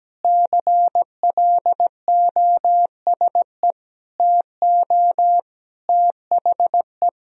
Morse code, heard as a keyed tone, about 17 wpm, 700 Hz. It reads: CLOSE TO THE
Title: Sentences from Top 300 Words and Abbreviations 17wpm